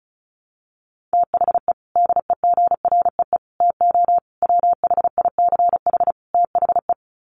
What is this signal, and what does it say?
Morse code, heard as a keyed tone, about 34 wpm, 700 Hz. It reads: THE DEGREE TO WHICH THE
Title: Sets of 5 Words Encore2 35wpm